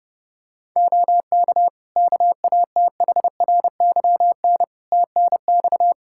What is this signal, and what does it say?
Morse code, heard as a keyed tone, about 32 wpm, 700 Hz. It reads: OK KATHRYN TNX
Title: K1USN SST QSO Practice 30wpm